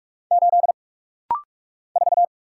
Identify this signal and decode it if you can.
Morse code, heard as a keyed tone, about 46 wpm, 700 Hz. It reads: 8 4
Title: Single Letter-Number Alt - Rapid-Fire 45wpm